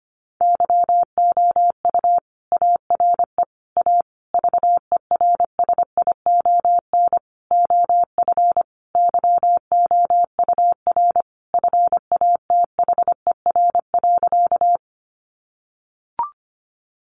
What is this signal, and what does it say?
Morse code, heard as a keyed tone, about 25 wpm, 700 Hz. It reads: YOU ARE A VERSION OF YOUR FATHER.
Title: Sentences from Top 1400 Words 25wpm